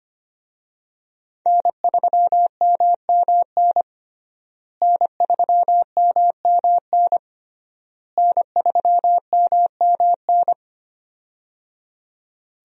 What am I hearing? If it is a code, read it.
Morse code, heard as a keyed tone, about 25 wpm, 700 Hz. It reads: N3MMN N3MMN N3MMN
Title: USA Call Signs - 5 Characters - Repeated 3x - 25wpm